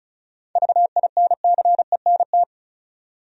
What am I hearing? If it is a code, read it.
Morse code, heard as a keyed tone, about 36 wpm, 700 Hz. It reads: VINCENT
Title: USA Common Names - 35wpm